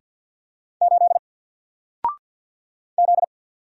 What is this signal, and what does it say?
Morse code, heard as a keyed tone, about 50 wpm, 700 Hz. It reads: 8 Z
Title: Single Letter-Number 50wpm